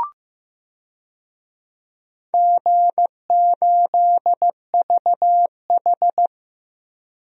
Morse code, heard as a keyed tone, about 15 wpm, 700 Hz. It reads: G8VH
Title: DXCC-100 Call Signs - 4 Characters - 15wpm